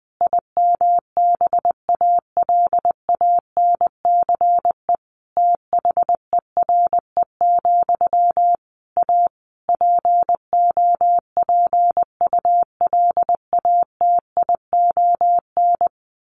Morse code, heard as a keyed tone, about 20 wpm, 700 Hz. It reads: IMBALANCE THERE, A POPULATION
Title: A Sound Of Thunder 20wpm